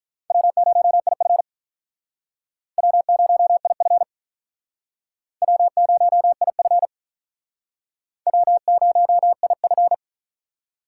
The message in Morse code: W0IF W0IF W0IF W0IF

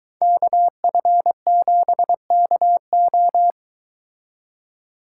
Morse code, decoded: KF7KO

